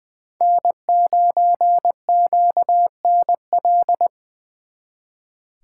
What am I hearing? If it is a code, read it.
Morse code, heard as a keyed tone, about 23 wpm, 700 Hz. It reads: N9QNL
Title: Speed Racing - USA Call Signs 20wpm